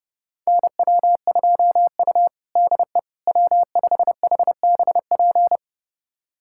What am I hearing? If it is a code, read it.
Morse code, heard as a keyed tone, about 30 wpm, 700 Hz. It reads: NW2U DE W5HBP
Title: USA Callsign DE Callsign 30wpm